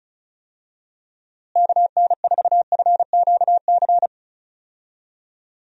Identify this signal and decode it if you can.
Morse code, heard as a keyed tone, about 34 wpm, 700 Hz. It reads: KN4FQC
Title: USA Call Signs - 6 Characters - Repeated 2x - 35wpm